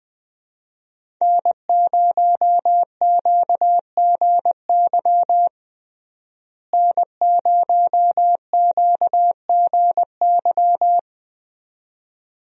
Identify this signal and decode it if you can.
Morse code, heard as a keyed tone, about 20 wpm, 700 Hz. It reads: N0QGY N0QGY
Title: USA Call Signs - 5 Characters - Repeated 2x - 20wpm